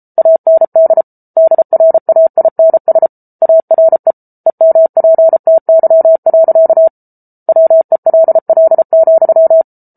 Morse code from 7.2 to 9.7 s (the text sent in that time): WELL,